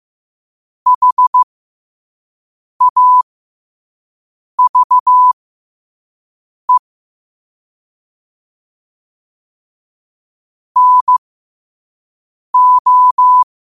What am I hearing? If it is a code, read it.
Morse code, heard as a keyed tone, about 15 wpm, spaced as for 5 wpm, 1000 Hz. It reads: HAVE NO
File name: qotdfiles / 5wpm